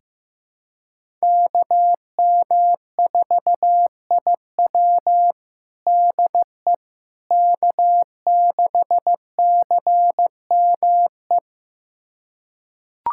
Morse code, heard as a keyed tone, about 15 wpm, 700 Hz. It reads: KM4IW DE K6CME
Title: USA Callsign DE Callsign - 5 Characters - 15wpm